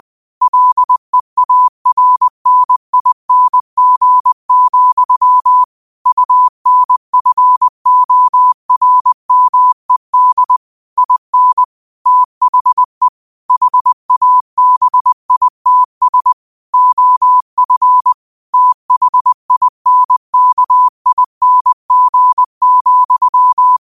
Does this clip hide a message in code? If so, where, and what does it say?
Yes: Morse code from 0.4 to 23.8 s, heard as a keyed tone, about 20 wpm, 1000 Hz. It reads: LEARNING, UNFORMED IN THE HABITS OF THINKING,